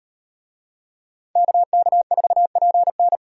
Morse code, heard as a keyed tone, about 38 wpm, 700 Hz. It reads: KK4PN